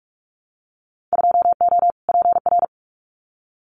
Morse code, heard as a keyed tone, about 44 wpm, 700 Hz. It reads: 2O PR
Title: Winter Field Day Exchanges Encore 45wpm